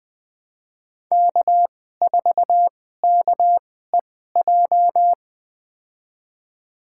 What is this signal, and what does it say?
Morse code, heard as a keyed tone, about 20 wpm, spaced as for 14 wpm, 700 Hz. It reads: K4KEJ
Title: USA Call Signs - 5 Characters - 2x Character Spacing - 20wpm